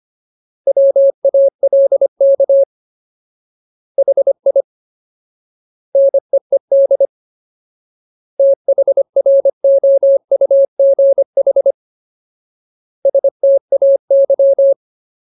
Morse code, heard as a keyed tone, about 25 wpm, 550 Hz. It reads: WALK HI NEED THROUGH STAY
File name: Ultimate Word List / 25WPM Ultimate